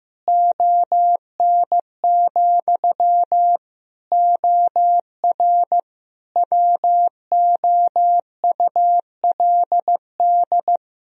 Morse code, heard as a keyed tone, about 16 wpm, 700 Hz. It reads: ON, OR WOULD
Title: Sentences from Top 1000 Words - Review Part 1 - 15wpm